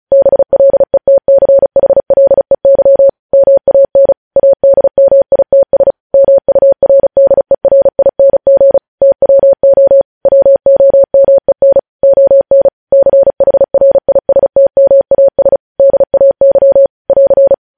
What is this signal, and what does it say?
Morse code, heard as a keyed tone, about 36 wpm, 550 Hz. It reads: BLETCHLEY MAN ADMITS MURDERING TWO WOMEN ON CHRISTMAS DAY <AR>
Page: news headlines in morse code at 35 wpm